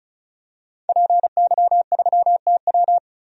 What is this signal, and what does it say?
Morse code, heard as a keyed tone, about 36 wpm, 700 Hz. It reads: PY3TW
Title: DXCC-100 Call Signs - 5 Characters - 35wpm